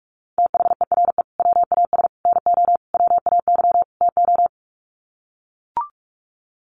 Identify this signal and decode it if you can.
Morse code, heard as a keyed tone, about 44 wpm, 700 Hz. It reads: THERE WAS NO WAY TO
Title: Sets of 5 Words Encore 45wpm